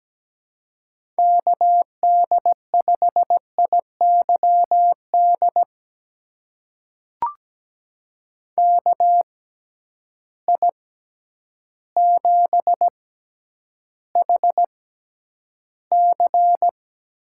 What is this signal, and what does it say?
Morse code, heard as a keyed tone, about 17 wpm, spaced as for 9 wpm, 700 Hz. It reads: KD5IYD KI7HC
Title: USA Call Signs - 6 Characters - 6x Character Spacing - 17wpm